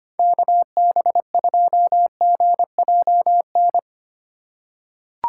Morse code, heard as a keyed tone, about 25 wpm, 700 Hz. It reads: KB2GJN
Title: USA Call Signs - 6 Characters - 3x Character Spacing - 25wpm